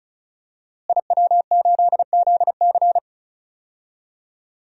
Morse code, heard as a keyed tone, about 36 wpm, 700 Hz. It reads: IW8ZC
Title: DXCC-100 Call Signs - 5 Characters - Repeated 3x - 35wpm